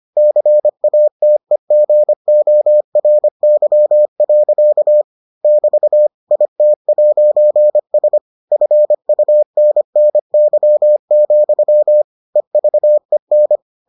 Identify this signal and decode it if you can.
Morse code, heard as a keyed tone, about 25 wpm, 600 Hz. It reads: CATEGORY. <BT> IT'S FUNNY, EVEN